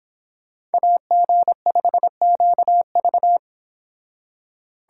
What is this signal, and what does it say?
Morse code, heard as a keyed tone, about 29 wpm, 700 Hz. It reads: AG5QV